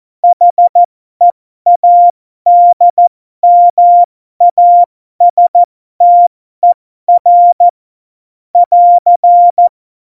Morse code, heard as a keyed tone, about 14 wpm, spaced as for 12 wpm, 700 Hz. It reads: HEADMASTER <AR>